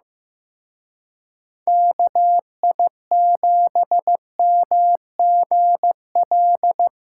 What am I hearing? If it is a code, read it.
Morse code, heard as a keyed tone, about 18 wpm, 700 Hz. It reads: KI7MGL
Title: Speed Racing - USA 2x3 Call Signs 15wpm